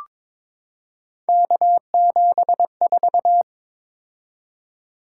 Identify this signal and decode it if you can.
Morse code, heard as a keyed tone, about 22 wpm, 700 Hz. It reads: K74
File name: practice / 22wpm / Three Letter-Number Combinations - 22wpm